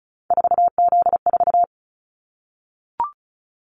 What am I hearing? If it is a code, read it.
Morse code, heard as a keyed tone, about 36 wpm, 690 Hz. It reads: 4Z4
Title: Callsign Prefixes for Top 50 DXCC - 35wpm